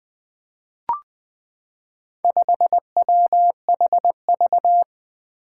Morse code, heard as a keyed tone, about 20 wpm, 700 Hz. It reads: 5WHV
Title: Four Challenging Letter-Number Combinations 20wpm